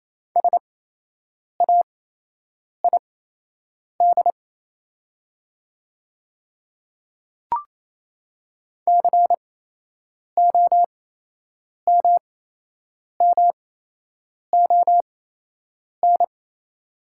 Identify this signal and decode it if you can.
Morse code, heard as a keyed tone, about 27 wpm, spaced as for 8 wpm, 700 Hz. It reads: SAID COMMON
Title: 8x Character Spacing - Top 500 Words - 28wpm